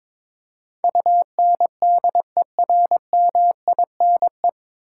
Morse code, heard as a keyed tone, about 22 wpm, 700 Hz. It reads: UNDERMINE
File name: 9 Letter Words - First 3 Letters Given - 22wpm